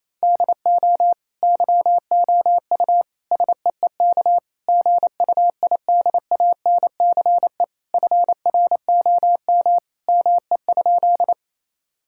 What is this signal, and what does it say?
Morse code, heard as a keyed tone, about 28 wpm, 700 Hz. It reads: DO YOU SEEK GUIDANCE FROM ME?
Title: Sentences from Top 1600 Words 28wpm